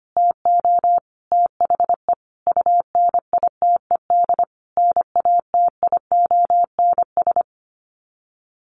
Morse code, heard as a keyed tone, about 25 wpm, 700 Hz. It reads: TO THE UNITED NATIONS
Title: Sets of 5 Words Encore2 25wpm